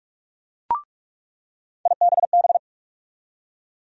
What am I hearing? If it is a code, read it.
Morse code, heard as a keyed tone, about 46 wpm, 700 Hz. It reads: IBB
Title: USA Call Sign Postfixes - 3 Letters 45wpm